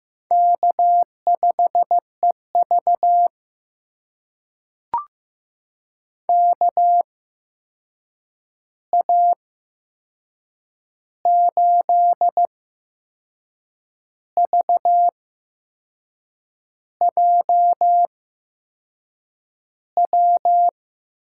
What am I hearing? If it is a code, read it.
Morse code, heard as a keyed tone, about 15 wpm, spaced as for 6 wpm, 700 Hz. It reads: K5EV KA8VJW
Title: USA Call Signs - 8x Character Spacing - 15wpm